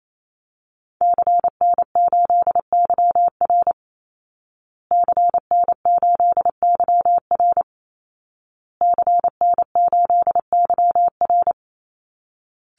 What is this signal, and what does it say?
Morse code, heard as a keyed tone, about 28 wpm, 700 Hz. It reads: CN8YR CN8YR CN8YR
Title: Call Signs Repeated 2x - 28wpm